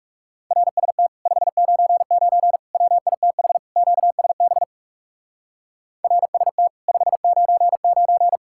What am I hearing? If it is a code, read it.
Morse code, heard as a keyed tone, about 38 wpm, 700 Hz. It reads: RST 599 WITH QSB RST 599